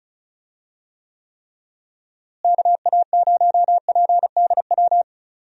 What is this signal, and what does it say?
Morse code, heard as a keyed tone, about 36 wpm, 700 Hz. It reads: KA0PDW
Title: USA Call Signs - 6 Characters - Repeated 3x - 35wpm